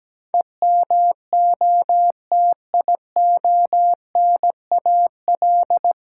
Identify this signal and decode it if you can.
Morse code, heard as a keyed tone, about 17 wpm, 700 Hz. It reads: EMOTIONAL
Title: Sentences from Top 1000 Words - Review Part 5 - New Word Drill - 17wpm